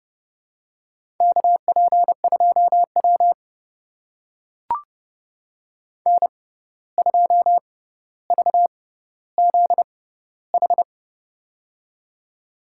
Morse code, heard as a keyed tone, about 31 wpm, spaced as for 15 wpm, 700 Hz. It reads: KP2W N2VZH
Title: USA Call Signs - 6x Character Spacing - 30wpm